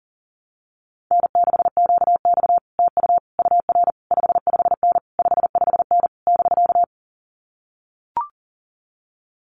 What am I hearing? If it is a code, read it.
Morse code, heard as a keyed tone, about 42 wpm, 700 Hz. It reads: N6QX TU UR 55N 55N <BK>
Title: Common QSO Phrases Encore 40wpm